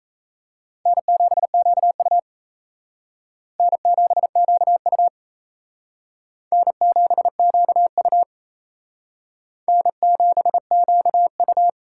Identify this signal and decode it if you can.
Morse code, heard as a keyed tone, about 35 wpm, 700 Hz. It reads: N7QU N7QU N7QU N7QU
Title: Speed Racing - USA 1x2 Call Signs 28wpm